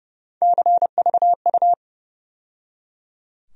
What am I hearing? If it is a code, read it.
Morse code, heard as a keyed tone, about 30 wpm, 700 Hz. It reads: CVU
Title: USA Call Sign Postfixes - 3 Letters 30wpm